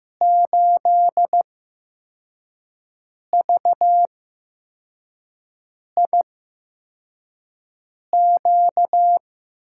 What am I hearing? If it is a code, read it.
Morse code, heard as a keyed tone, about 15 wpm, spaced as for 4 wpm, 700 Hz. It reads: 8VIQ
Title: USA Call Signs - 6 Characters - 8x Character Spacing - 15wpm